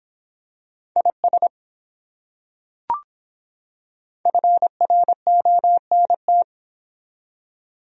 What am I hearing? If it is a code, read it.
Morse code, heard as a keyed tone, about 24 wpm, 700 Hz. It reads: IS FRONT